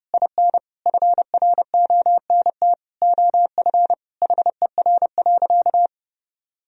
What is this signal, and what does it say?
Morse code, heard as a keyed tone, about 32 wpm, 700 Hz. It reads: IN FRONT OF HER.
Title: Sentences from Top 1000 Words - Review Part 2 - 30wpm